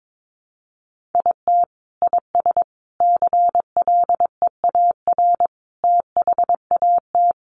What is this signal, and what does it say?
Morse code, heard as a keyed tone, about 22 wpm, 700 Hz. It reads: IT IS CLEAR THAT
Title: Sets of 5 Words Encore 22wpm